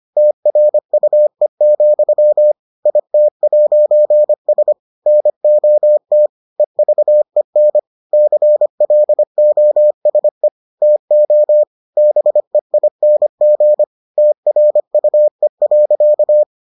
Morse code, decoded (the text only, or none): TRUE, IT'S NOT EVEN CLOSE TO BEING TRUE.